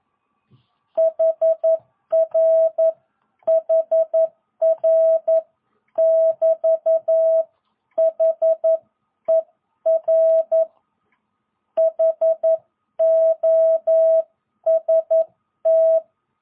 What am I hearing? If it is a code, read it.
Morse code, heard as a keyed tone, about 11 wpm, 650 Hz. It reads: HR HR <BT> HER HOST
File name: jh8cbh / 2022cw